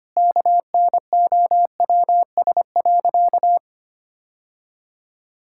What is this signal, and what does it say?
Morse code, heard as a keyed tone, about 25 wpm, 700 Hz. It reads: KNOWS.
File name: practice / 25wpm / Sentences with 2-Word Phrases - 25wpm